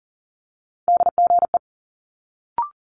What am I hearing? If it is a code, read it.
Morse code, heard as a keyed tone, about 39 wpm, 700 Hz. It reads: DGE
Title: Multi-letter Phonemes Quick Fire 40wpm